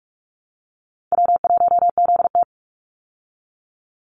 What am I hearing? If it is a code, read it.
Morse code, heard as a keyed tone, about 44 wpm, 700 Hz. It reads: W1ZT
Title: Call Signs Repeated 2x - 45wpm